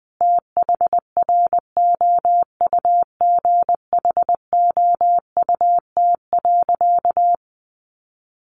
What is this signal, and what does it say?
Morse code, heard as a keyed tone, about 20 wpm, 700 Hz. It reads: THROUGHOUT.
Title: Sentences from Top 800 Words 20wpm